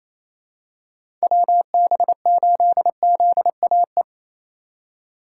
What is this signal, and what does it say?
Morse code, heard as a keyed tone, about 28 wpm, 700 Hz. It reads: WB8ZAE
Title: USA Call Signs - 6 Characters - 4x Character Spacing - 28wpm